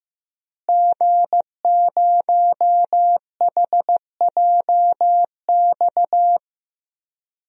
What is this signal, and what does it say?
Morse code, heard as a keyed tone, about 15 wpm, 700 Hz. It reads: G0HJX
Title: DXCC-100 Call Signs - 5 Characters - Repeated 2x - 15wpm